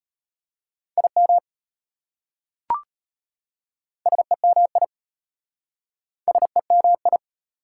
Morse code, heard as a keyed tone, about 33 wpm, 690 Hz. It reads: IM SEMI SEMI